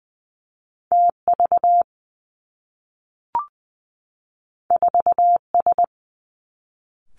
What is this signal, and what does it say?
Morse code, heard as a keyed tone, about 20 wpm, 700 Hz. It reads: T V 4 S
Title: Two Challenging Letter-Number Combinations 20wpm